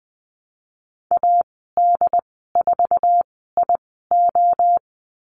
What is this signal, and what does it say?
Morse code, heard as a keyed tone, about 20 wpm, spaced as for 15 wpm, 700 Hz. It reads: AD4IO
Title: 128 Lesson 21g - Number 6 - Call Signs 20wpm